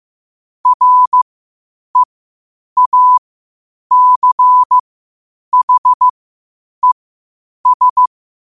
Morse code, heard as a keyed tone, about 15 wpm, spaced as for 8 wpm, 1000 Hz. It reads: REACHES
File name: qotdfiles / 8wpm